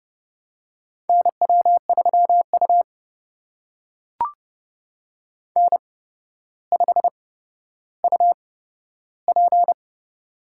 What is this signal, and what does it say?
Morse code, heard as a keyed tone, about 32 wpm, spaced as for 13 wpm, 700 Hz. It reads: NW3U N5UP
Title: USA Call Signs - 4 Characters - 8x Character Spacing - 30wpm